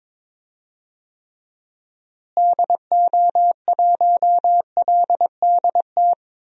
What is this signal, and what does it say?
Morse code, heard as a keyed tone, about 21 wpm, 700 Hz. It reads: DO1LDT
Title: DXCC-100 Call Signs - 6 Characters - Repeated 2x - 22wpm